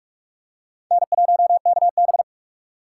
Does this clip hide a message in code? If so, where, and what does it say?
Yes: Morse code from 0.9 to 2.2 s, heard as a keyed tone, about 48 wpm, 700 Hz. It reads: N1KB